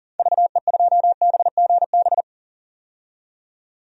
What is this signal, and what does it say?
Morse code, heard as a keyed tone, about 40 wpm, 700 Hz. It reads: VE2BGB
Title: DXCC-100 Call Signs - 6 Characters - 40wpm